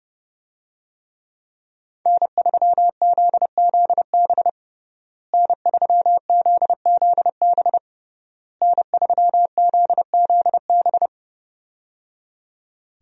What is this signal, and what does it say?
Morse code, heard as a keyed tone, about 28 wpm, 700 Hz. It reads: N3ZZB N3ZZB N3ZZB
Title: USA Call Signs - 5 Characters - Repeated 3x - 30wpm